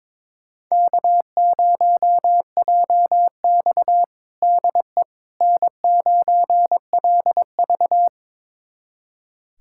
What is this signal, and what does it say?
Morse code, heard as a keyed tone, about 23 wpm, 700 Hz. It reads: K0JX DE N9LV
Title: USA Callsign DE Callsign - 4 Characters - 22wpm